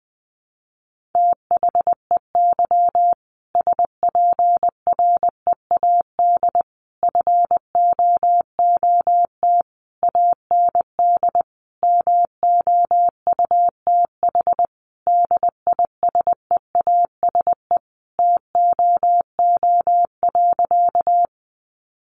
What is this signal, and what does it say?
Morse code, heard as a keyed tone, about 20 wpm, 700 Hz. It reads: THEY SPREAD FOOT AND MOUTH DISEASE TOO.
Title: Sentences from Top 1500 Words 20wpm